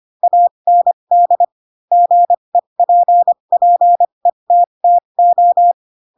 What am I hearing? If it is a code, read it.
Morse code, heard as a keyed tone, about 25 wpm, spaced as for 22 wpm, 700 Hz. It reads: AND GEPPETTO
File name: Pinocchio / Pinocchio ch3 18 25 ik1hge